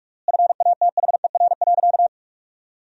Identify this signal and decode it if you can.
Morse code, heard as a keyed tone, about 41 wpm, 700 Hz. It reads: FATHER.
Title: 3x Word-Spacing - Sentences from Top 500 Words - 45wpm